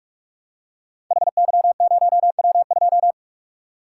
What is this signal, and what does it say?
Morse code, heard as a keyed tone, about 44 wpm, 700 Hz. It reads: HY0W2
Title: Code Groups - 5 Letter:Numbers - 45wpm